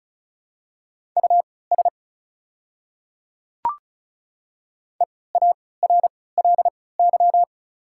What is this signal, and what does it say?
Morse code, heard as a keyed tone, about 35 wpm, spaced as for 20 wpm, 690 Hz. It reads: US EARLY
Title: 3x Character Spacing - Top 500 Words - 35wpm